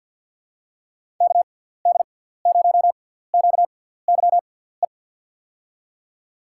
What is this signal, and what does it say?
Morse code, heard as a keyed tone, about 52 wpm, spaced as for 17 wpm, 700 Hz. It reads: KD0QYE
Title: USA Call Signs - 6 Characters - 6x Character Spacing - 50wpm